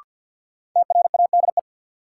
Morse code, heard as a keyed tone, about 50 wpm, 700 Hz. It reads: TRADE
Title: Sets of 5 Words Encore - New Word Drill - 50wpm